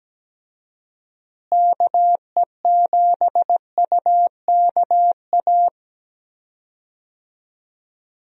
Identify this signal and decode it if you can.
Morse code, heard as a keyed tone, about 17 wpm, 700 Hz. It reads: KE7UKA
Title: USA Call Signs - 6 Characters - Repeated 2x - 17wpm